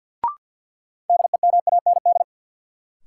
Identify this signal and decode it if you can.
Morse code, heard as a keyed tone, about 46 wpm, 700 Hz. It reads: DEMAND
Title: Sentences from Top 800 Words - New Word Drill - 50wpm